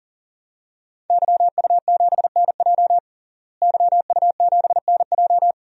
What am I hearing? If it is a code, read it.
Morse code, heard as a keyed tone, about 37 wpm, 700 Hz. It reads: YU7NJ YU7NJ
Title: DXCC-100 Call Signs - 5 Characters - Repeated 2x - 40wpm